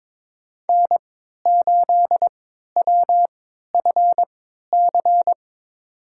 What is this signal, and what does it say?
Morse code, heard as a keyed tone, about 21 wpm, spaced as for 12 wpm, 700 Hz. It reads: N8WFC
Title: USA Call Signs - 5 Characters - 3x Character Spacing - 22wpm